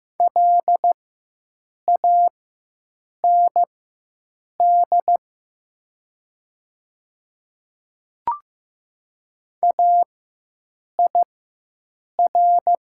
Morse code, heard as a keyed tone, about 15 wpm, spaced as for 7 wpm, 700 Hz. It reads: LAND AIR
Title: 4x Character Spacing - Top 500 Words - 15wpm